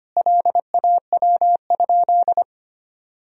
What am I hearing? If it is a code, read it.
Morse code, heard as a keyed tone, about 25 wpm, 700 Hz. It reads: LAW?
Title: Sentences from Top 1000 Words - Review Part 4 - 25wpm